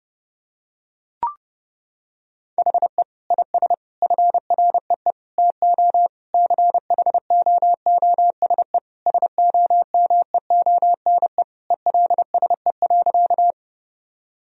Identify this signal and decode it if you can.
Morse code, heard as a keyed tone, about 32 wpm, 700 Hz. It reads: HE IS FREE TO CHOOSE SOMEONE ELSE.
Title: Sentences from Top 1000 Words - Review Part 4 - 30wpm